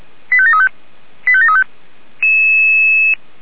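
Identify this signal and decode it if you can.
Five-tone selective call (ZVEI), heard as quick runs of five tones, tones of about 70 ms. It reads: 85735 85735
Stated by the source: ZVEI five-tone selective call 85741 received by SDR.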